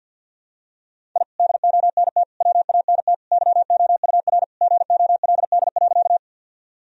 Morse code, heard as a keyed tone, about 50 wpm, 700 Hz. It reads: I DONT WANT YOUR GOLD.
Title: Sentences from Top 1000 Words - Review Part 3 - 50wpm